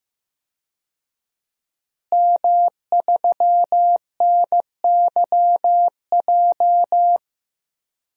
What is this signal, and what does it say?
Morse code, heard as a keyed tone, about 15 wpm, 700 Hz. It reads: M3NYJ
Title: DXCC-100 Call Signs - 5 Characters - Repeated 2x - 15wpm